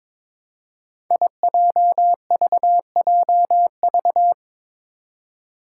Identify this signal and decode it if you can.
Morse code, heard as a keyed tone, about 22 wpm, 700 Hz. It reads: IJVJV
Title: Five Challenging Letter-Number Combinations 22wpm